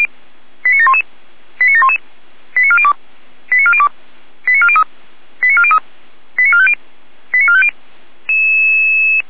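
Five-tone selective call (ZVEI), heard as quick runs of five tones, tones of about 80 ms. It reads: 89711 89711 89402 89402 89403 89403 89450 89450
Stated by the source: Received on SDR, ZVEI five-tone call 89735.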